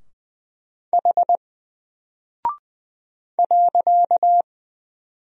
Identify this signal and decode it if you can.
Morse code, heard as a keyed tone, about 20 wpm, 700 Hz. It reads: H .
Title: Single Character (Letter:Number:Punctuation:Prosign) 20wpm